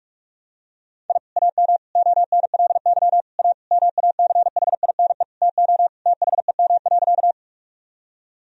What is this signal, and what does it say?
Morse code, heard as a keyed tone, about 46 wpm, 700 Hz. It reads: I AM ONLY A MACHINE TO THEM.
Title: Sentences from Top 1000 Words - Review Part 2 - 45wpm